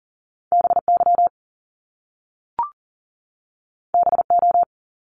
Morse code, heard as a keyed tone, about 38 wpm, 700 Hz. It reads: BY BO
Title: Top Two Letter Combinations in Words 40wpm